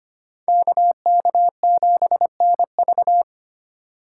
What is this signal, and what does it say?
Morse code, heard as a keyed tone, about 32 wpm, 700 Hz. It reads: KK7NV